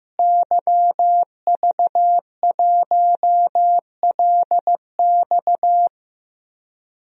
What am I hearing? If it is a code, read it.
Morse code, heard as a keyed tone, about 15 wpm, 700 Hz. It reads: YV1LX
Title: DXCC-100 Call Signs - 5 Characters - Repeated 2x - 15wpm